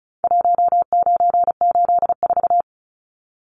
Morse code, heard as a keyed tone, about 34 wpm, 700 Hz. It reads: 1984
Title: Numbers - Contest Style - 4-Digit 35wpm